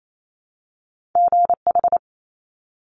Morse code, heard as a keyed tone, about 28 wpm, 700 Hz. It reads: GH
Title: Top Two Letter Combinations in Words 28wpm